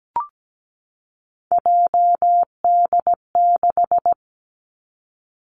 Morse code, heard as a keyed tone, about 17 wpm, 700 Hz. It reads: JD6
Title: Three Challenging Letter-Number Combinations 17wpm